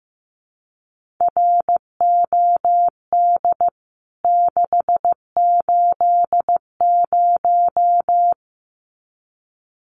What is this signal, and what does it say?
Morse code, heard as a keyed tone, about 15 wpm, 700 Hz. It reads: ROD 680
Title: CWT Exchange 15wpm